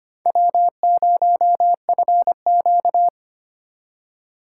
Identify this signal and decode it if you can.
Morse code, heard as a keyed tone, about 25 wpm, 700 Hz. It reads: W0FQ
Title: USA Call Signs - 2x Character Spacing - 25wpm